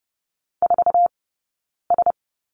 Morse code, heard as a keyed tone, about 30 wpm, 700 Hz. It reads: 4 S
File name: Lesson 19d - Letter w - Two Characters 30wpm